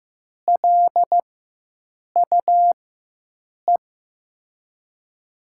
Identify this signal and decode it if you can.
Morse code, heard as a keyed tone, about 15 wpm, spaced as for 7 wpm, 700 Hz. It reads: LUE
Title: USA Call Signs - 6 Characters - 4x Character Spacing - 15wpm